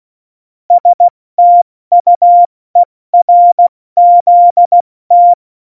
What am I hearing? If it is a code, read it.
Morse code, heard as a keyed tone, about 16 wpm, 700 Hz. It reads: STUERZT
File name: Morsekurs / Witze / Witz 06. Februar 2018, 70 F80